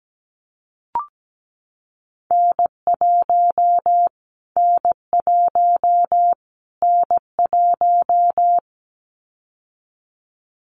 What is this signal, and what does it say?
Morse code, heard as a keyed tone, about 17 wpm, 700 Hz. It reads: N1 N1 N1
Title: Callsign Prefixes for Top 25 DXCC - Repeated 3x - 17wpm